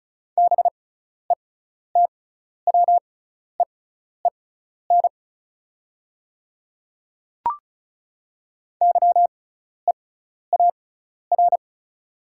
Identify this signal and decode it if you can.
Morse code, heard as a keyed tone, about 36 wpm, spaced as for 12 wpm, 700 Hz. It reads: BETWEEN YEAR
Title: 6x Character Spacing - Top 300 Words - 35wpm